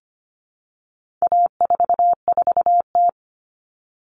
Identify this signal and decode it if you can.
Morse code, heard as a keyed tone, about 25 wpm, 700 Hz. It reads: A44T
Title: All Cut Numbers - 4-Digit 25wpm